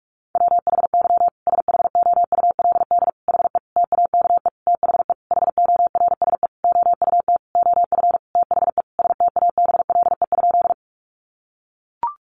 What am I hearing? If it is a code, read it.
Morse code, heard as a keyed tone, about 44 wpm, 700 Hz. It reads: WHY SHOULD HE TAKE THE HORSE OUT OF THE STABLE?
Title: Sentences from Top 1500 Words 45wpm